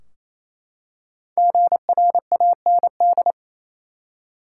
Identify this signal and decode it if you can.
Morse code, heard as a keyed tone, about 28 wpm, 700 Hz. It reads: GRAND
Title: Five Letter Words - 28wpm